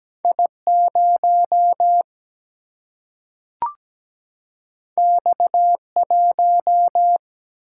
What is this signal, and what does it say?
Morse code, heard as a keyed tone, about 17 wpm, 700 Hz. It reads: I 0 X 1
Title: Single-Letter Single-Number 17wpm